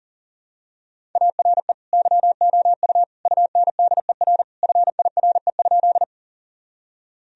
Morse code, heard as a keyed tone, about 40 wpm, 700 Hz. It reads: ARE YOU UNDER FIRE?
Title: Sentences from Top 1000 Words - Review Part 1 - 40wpm